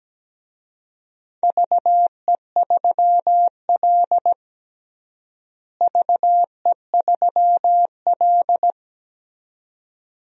Morse code, heard as a keyed tone, about 17 wpm, 700 Hz. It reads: VE3L VE3L
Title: DXCC-100 Call Signs - 4 Characters - Repeated 2x - 17wpm